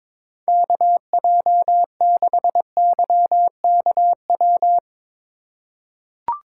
Morse code, heard as a keyed tone, about 22 wpm, 700 Hz. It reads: KJ6YKW
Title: USA Call Signs - 6 Characters - 3x Character Spacing - 22wpm